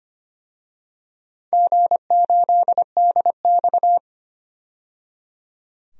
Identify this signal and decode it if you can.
Morse code, heard as a keyed tone, about 25 wpm, 700 Hz. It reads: G8DX
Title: DXCC-100 Call Signs - 4 Characters - 25wpm